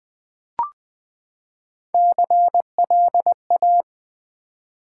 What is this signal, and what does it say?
Morse code, heard as a keyed tone, about 20 wpm, 700 Hz. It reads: CLA
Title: Build A Word - 3 Letters At A Time - 9-Letter Words - 20wpm